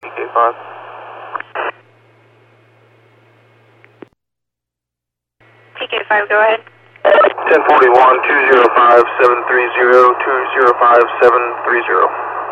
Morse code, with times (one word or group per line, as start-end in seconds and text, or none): none